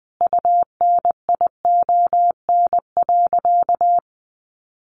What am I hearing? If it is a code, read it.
Morse code, heard as a keyed tone, about 20 wpm, 700 Hz. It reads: UNION.
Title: Sentences from Top 1400 Words 20wpm